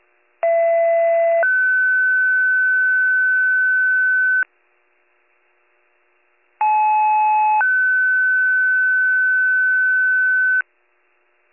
Two-tone page, heard as a one-second tone, then a three-second tone, about 670 then 1500 Hz, about 860 then 1500 Hz. It is heard from 0.4 to 10.6 s.